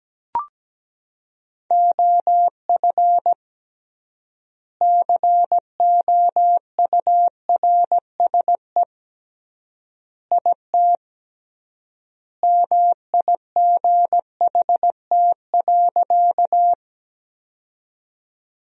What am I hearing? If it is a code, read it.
Morse code, heard as a keyed tone, about 17 wpm, 700 Hz. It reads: OF COURSE IT MIGHT.
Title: 3x Word-Spacing - Sentences from Top 500 Words - 17wpm